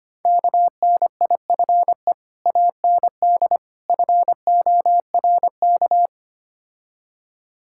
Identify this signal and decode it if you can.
Morse code, heard as a keyed tone, about 26 wpm, 690 Hz. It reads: KNIFE AND FORK
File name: Easy Words - Speed-Racing with Hints - 25wpm